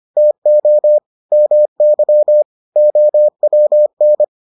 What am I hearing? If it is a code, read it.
Morse code, heard as a keyed tone, about 25 wpm, 600 Hz. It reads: TO MY OWN